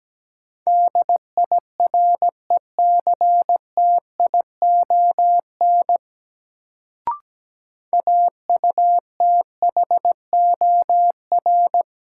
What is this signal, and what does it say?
Morse code, heard as a keyed tone, about 17 wpm, 700 Hz. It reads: DIRECTION AUTHOR
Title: Sets of 4 Words - New Word Drill - 17wpm